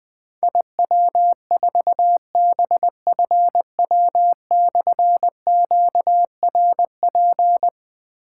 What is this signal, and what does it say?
Morse code, heard as a keyed tone, about 20 wpm, 700 Hz. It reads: IW4BFW/QRP
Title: Difficult Call Signs 20wpm